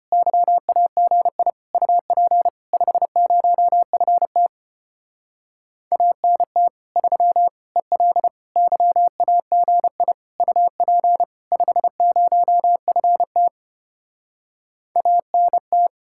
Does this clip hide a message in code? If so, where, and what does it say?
Yes: Morse code from 0.1 to 15.9 s, heard as a keyed tone, about 33 wpm, 700 Hz. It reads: YAGI UP 50FT ANT 3 EL YAGI UP 50FT ANT